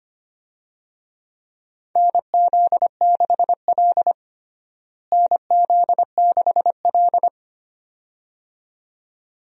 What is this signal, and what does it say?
Morse code, heard as a keyed tone, about 25 wpm, 700 Hz. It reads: NZ6L NZ6L
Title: USA Call Signs - 4 Characters - Repeated 2x - 25wpm